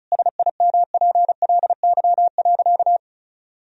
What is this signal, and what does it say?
Morse code, heard as a keyed tone, about 34 wpm, 700 Hz. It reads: SIMPLY.
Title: Sentences from Top 1000 Words - Review Part 4 - 35wpm